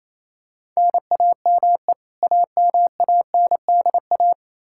Morse code, heard as a keyed tone, about 28 wpm, 700 Hz. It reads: NAME AMANDA
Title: Practice QSO - Exchange 1 - Beginning - 28wpm